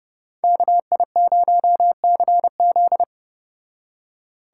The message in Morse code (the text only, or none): KI0CZ